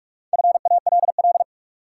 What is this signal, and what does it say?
Morse code, heard as a keyed tone, about 46 wpm, 700 Hz. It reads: FALL
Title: Sentences from Top 500 Words - New Word Drill - 45wpm